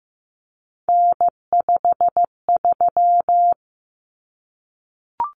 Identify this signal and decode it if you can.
Morse code, heard as a keyed tone, about 15 wpm, 700 Hz. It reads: N53
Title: Common Cut Numbers - 3-Digit 15wpm